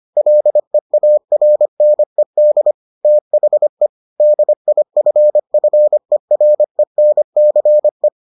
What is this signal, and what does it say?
Morse code, heard as a keyed tone, about 25 wpm, 600 Hz. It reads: LEARNED THE DIFFERENCE